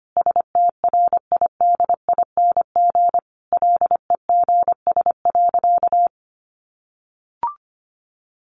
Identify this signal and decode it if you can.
Morse code, heard as a keyed tone, about 25 wpm, 700 Hz. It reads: STRIDING LEGS.
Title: A Sound Of Thunder 25wpm